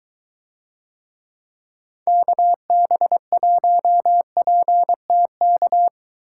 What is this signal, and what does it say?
Morse code, heard as a keyed tone, about 21 wpm, 700 Hz. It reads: KB1PTK